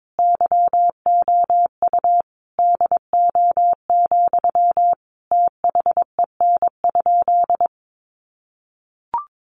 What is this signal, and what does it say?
Morse code, heard as a keyed tone, about 22 wpm, 700 Hz. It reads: YOU DO, THEN?
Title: Sentences from Top 100 Words 22wpm